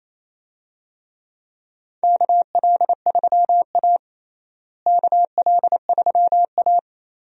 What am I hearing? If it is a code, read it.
Morse code, heard as a keyed tone, about 28 wpm, 700 Hz. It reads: KL3A KL3A
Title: USA Call Signs - 4 Characters - Repeated 2x - 28wpm